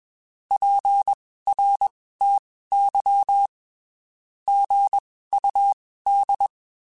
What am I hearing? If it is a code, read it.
Morse code, heard as a keyed tone, about 21 wpm, spaced as for 15 wpm, 800 Hz. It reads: PRTY GUD
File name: (spaced) / QSOs Spaced-mp3s / WA7QOX 1 @20 sp